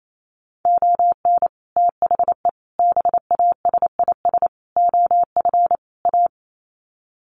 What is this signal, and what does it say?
Morse code, heard as a keyed tone, about 28 wpm, 700 Hz. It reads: ON THE BASIS OF A
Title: Sets of 5 Words Encore 28wpm